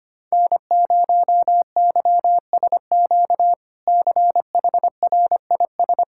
Morse code, heard as a keyed tone, about 25 wpm, 700 Hz. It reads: N0YSQ CHRIS
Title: K1USN SST QSO Practice - 25wpm